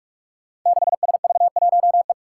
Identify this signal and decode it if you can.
Morse code, heard as a keyed tone, about 44 wpm, 700 Hz. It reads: 6SV1E
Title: Five Challenging Letter-Number Combinations 45wpm